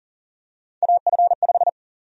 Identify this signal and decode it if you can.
Morse code, heard as a keyed tone, about 40 wpm, 700 Hz. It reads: AF5
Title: USA Call Sign Prefixes - Repeated 2x - 40wpm